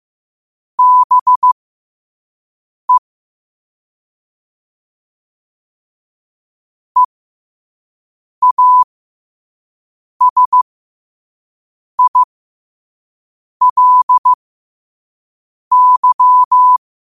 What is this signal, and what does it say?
Morse code, heard as a keyed tone, about 15 wpm, spaced as for 5 wpm, 1000 Hz. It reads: BE EASILY
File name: qotdfiles / 5wpm